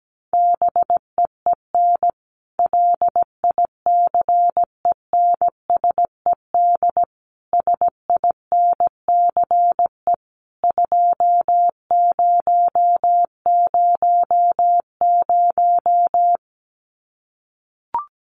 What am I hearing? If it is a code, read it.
Morse code, heard as a keyed tone, about 17 wpm, 700 Hz. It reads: BEEN LICENSED SINCE 2000
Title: Common QSO Phrases 17wpm